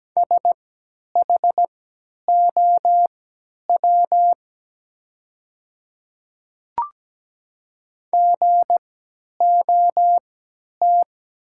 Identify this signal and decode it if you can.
Morse code, heard as a keyed tone, about 17 wpm, spaced as for 10 wpm, 690 Hz. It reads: SHOW GOT
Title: 3x Character Spacing - Top 500 Words - 17wpm